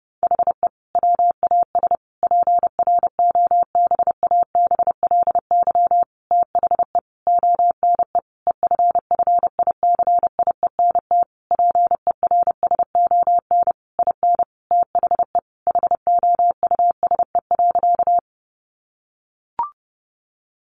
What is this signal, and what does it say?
Morse code, heard as a keyed tone, about 30 wpm, 700 Hz. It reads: HE WAS PROBABLY THE ONE EFFICIENT PERSON IN THE HOUSE.
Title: Sentences from Top 1300 Words 30wpm